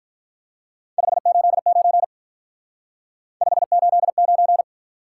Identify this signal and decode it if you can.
Morse code, heard as a keyed tone, about 44 wpm, 700 Hz. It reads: 589 589